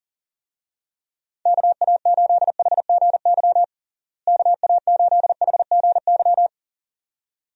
Morse code, heard as a keyed tone, about 40 wpm, 700 Hz. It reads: KA8HGY KA8HGY
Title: USA Call Signs - 6 Characters - Repeated 2x - 40wpm